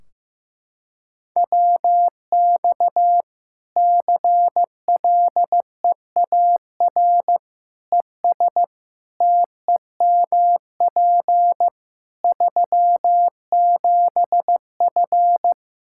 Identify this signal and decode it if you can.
Morse code, heard as a keyed tone, about 15 wpm, 700 Hz. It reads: WX CLEAR ES TEMP 37F
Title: K1USN SST QSO Practice 15wpm